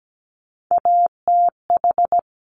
Morse code, heard as a keyed tone, about 17 wpm, 700 Hz. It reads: ATH
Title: Top Three Letter Combinations in Words 17wpm